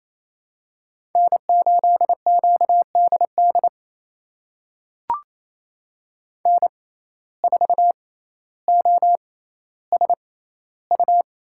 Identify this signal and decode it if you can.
Morse code, heard as a keyed tone, about 28 wpm, spaced as for 14 wpm, 700 Hz. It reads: N8QDD N4OSU
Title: USA Call Signs - 5 Characters - 6x Character Spacing - 28wpm